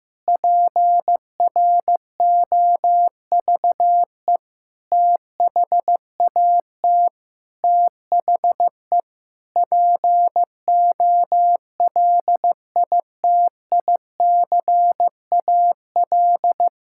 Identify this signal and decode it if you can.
Morse code, heard as a keyed tone, about 15 wpm, 700 Hz. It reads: PROVE THAT THE POLITICAL
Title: Sentences from Top 1600 Words 15wpm